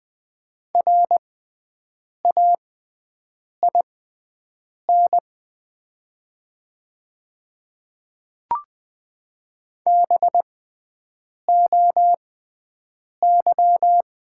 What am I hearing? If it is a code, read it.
Morse code, heard as a keyed tone, about 20 wpm, spaced as for 7 wpm, 700 Hz. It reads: RAIN BOY
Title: 6x Character Spacing - Top 500 Words - 20wpm